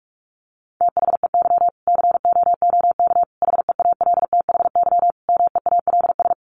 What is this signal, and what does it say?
Morse code, heard as a keyed tone, about 44 wpm, 700 Hz. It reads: THEY COOK HEALTHY MEALS
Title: Easy Phrases 45wpm